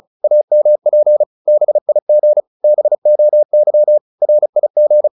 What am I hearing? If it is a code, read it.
Morse code, heard as a keyed tone, about 34 wpm, 600 Hz. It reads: AMP BIG BOY RIG